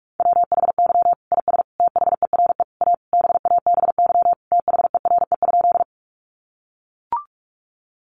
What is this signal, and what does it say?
Morse code, heard as a keyed tone, about 44 wpm, 700 Hz. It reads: WHY IS THERE A BABY THERE?
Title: Sentences from Top 1300 Words 45wpm